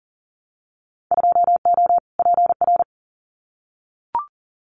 Morse code, heard as a keyed tone, about 42 wpm, 700 Hz. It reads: 2O PR
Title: Winter Field Day Exchanges Encore 40wpm